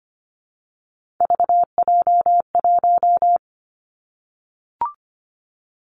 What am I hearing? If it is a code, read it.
Morse code, heard as a keyed tone, about 25 wpm, 700 Hz. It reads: VJ1
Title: Three Challenging Letter-Number Combinations 25wpm